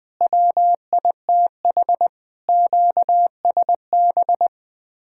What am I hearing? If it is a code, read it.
Morse code, heard as a keyed tone, about 20 wpm, 700 Hz. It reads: WITH QSB
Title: Speed-Racing Common QSO Phrases Encore - 20wpm